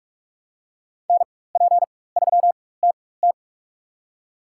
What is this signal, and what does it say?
Morse code, heard as a keyed tone, about 46 wpm, spaced as for 21 wpm, 700 Hz. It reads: NP3TT
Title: USA Call Signs - 5 Characters - 4x Character Spacing - 45wpm